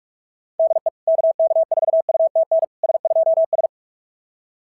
Morse code, heard as a keyed tone, about 44 wpm, 690 Hz. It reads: DE KK4UTN S2S
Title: SOTA QSO Practice 45wpm